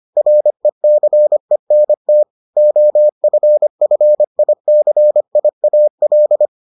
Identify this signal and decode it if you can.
Morse code, heard as a keyed tone, about 25 wpm, 600 Hz. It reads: RECENT OFFICIAL